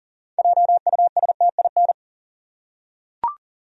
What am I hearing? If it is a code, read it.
Morse code, heard as a keyed tone, about 40 wpm, 700 Hz. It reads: JUSTIN
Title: USA Common Names - 40wpm